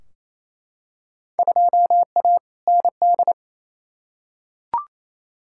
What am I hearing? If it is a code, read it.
Morse code, heard as a keyed tone, about 28 wpm, 700 Hz. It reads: 2A ND
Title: Field Day Exchanges Encore - 28wpm